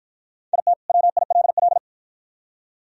Morse code, heard as a keyed tone, about 53 wpm, 700 Hz. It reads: IT WILL